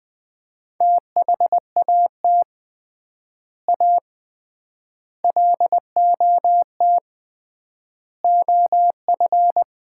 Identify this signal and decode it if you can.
Morse code, heard as a keyed tone, about 20 wpm, 700 Hz. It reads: THAT A LOT OF
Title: 3x Word-Spacing - Sets of 5 Words - 20wpm